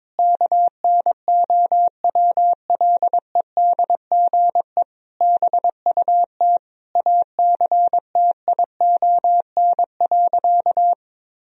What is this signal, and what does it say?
Morse code, heard as a keyed tone, about 22 wpm, 700 Hz. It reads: KNOWLEDGE BUT ACTION.
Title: Sentences from Top 1000 Words - Review Part 4 - 22wpm